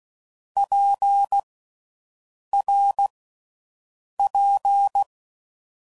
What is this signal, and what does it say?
Morse code, heard as a keyed tone, about 16 wpm, 800 Hz. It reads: P R P
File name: Groups / Basic Letter Groups / RKPX